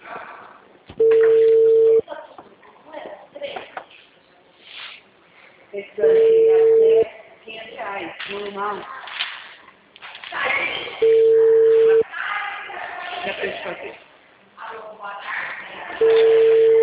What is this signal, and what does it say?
Ringback tone (China, 450 Hz), with speech.
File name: out- -2017- - - .51038